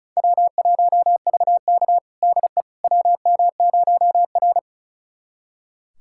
Morse code, heard as a keyed tone, about 34 wpm, 700 Hz. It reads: W1VK DE WM0R
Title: USA Callsign DE Callsign - 4 Characters - 35wpm